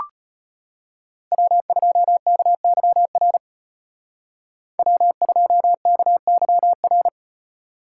Morse code, heard as a keyed tone, about 34 wpm, 700 Hz. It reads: W2KYR W2KYR